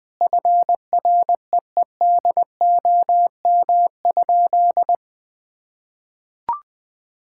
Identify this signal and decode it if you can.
Morse code, heard as a keyed tone, about 21 wpm, 700 Hz. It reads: FREEDOM?
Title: Sentences from Top 1000 Words - Review Part 5 - 20wpm